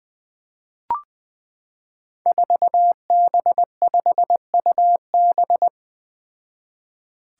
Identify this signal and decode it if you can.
Morse code, heard as a keyed tone, about 20 wpm, 700 Hz. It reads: 4B5UB
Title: Five Challenging Letter-Number Combinations 20wpm